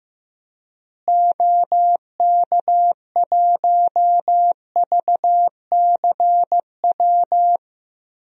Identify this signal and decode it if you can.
Morse code, heard as a keyed tone, about 15 wpm, 700 Hz. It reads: OK1VCW
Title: DXCC-100 Call Signs - 6 Characters - Repeated 3x - 15wpm